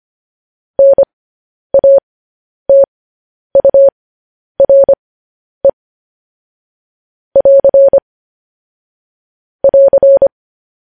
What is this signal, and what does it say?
Morse code, heard as a keyed tone, about 25 wpm, spaced as for 10 wpm, 550 Hz. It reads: NATURE <AR> <AR>
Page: news-headlines-in-morse-code-at-10-wpm